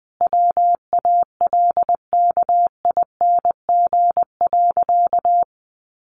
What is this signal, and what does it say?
Morse code, heard as a keyed tone, about 20 wpm, 700 Hz. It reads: WALKING.
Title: Easy Phrases 20wpm